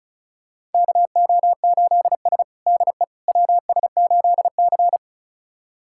Morse code, heard as a keyed tone, about 36 wpm, 700 Hz. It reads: KO8S DE WS8C
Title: USA Callsign DE Callsign - 4 Characters - 35wpm